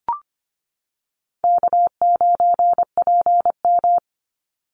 Morse code, heard as a keyed tone, about 25 wpm, 700 Hz. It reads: K9PM
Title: NA SOTA Chaser Callsigns 25wpm